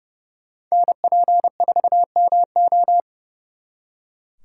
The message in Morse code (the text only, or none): NP4MO